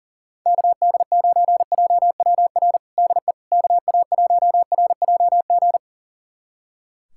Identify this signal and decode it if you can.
Morse code, heard as a keyed tone, about 40 wpm, 700 Hz. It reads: KD9JWR DE KA1RJG
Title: USA Callsign DE Callsign 40wpm